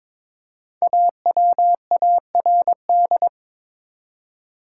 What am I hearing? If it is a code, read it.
Morse code, heard as a keyed tone, about 22 wpm, 700 Hz. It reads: AWARD
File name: 5 Letter Words - First 3 Letters Given - 22wpm